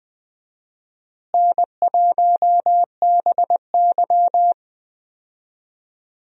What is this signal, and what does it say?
Morse code, heard as a keyed tone, about 20 wpm, 700 Hz. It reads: N1BY
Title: USA Call Signs - 4 Characters - Repeated 2x - 20wpm